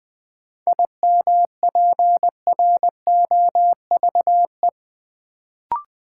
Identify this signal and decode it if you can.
Morse code, heard as a keyed tone, about 20 wpm, 700 Hz. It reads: IMPROVE
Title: Sentences from Top 800 Words - New Word Drill - 20wpm